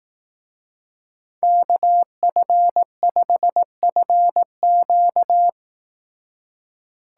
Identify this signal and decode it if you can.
Morse code, heard as a keyed tone, about 18 wpm, 700 Hz. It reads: KF5FQ
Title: Speed Racing - USA 2x2 Call Signs 15wpm